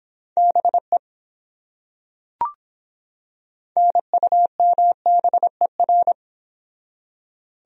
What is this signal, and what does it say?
Morse code, heard as a keyed tone, about 24 wpm, 700 Hz. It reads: BE NUMBER